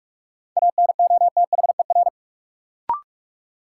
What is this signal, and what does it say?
Morse code, heard as a keyed tone, about 43 wpm, 700 Hz. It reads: ANOTHER
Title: Sets of 5 Words Encore - New Word Drill - 45wpm